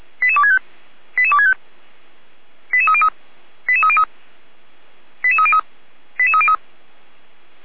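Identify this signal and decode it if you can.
Five-tone selective call (ZVEI), heard as quick runs of five tones, tones of about 70 ms. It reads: 88265 88265 88303 88303 88303 88303
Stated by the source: ZVEI five-tone selective call received on SDR.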